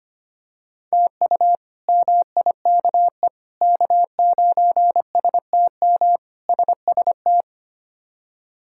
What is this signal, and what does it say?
Morse code, heard as a keyed tone, about 25 wpm, 700 Hz. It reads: TU MIKE K9STM SST
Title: K1USN SST QSO Practice - 25wpm